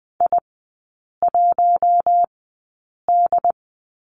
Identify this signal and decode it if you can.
Morse code, heard as a keyed tone, about 20 wpm, spaced as for 8 wpm, 700 Hz. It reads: I1D
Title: 074 Lesson 13e - Letter L - Three Characters 20wpm